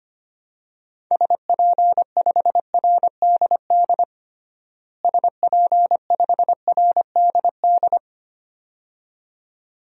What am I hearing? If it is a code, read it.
Morse code, heard as a keyed tone, about 25 wpm, 700 Hz. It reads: SP5RDD SP5RDD
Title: DXCC-100 Call Signs - 6 Characters - Repeated 2x - 25wpm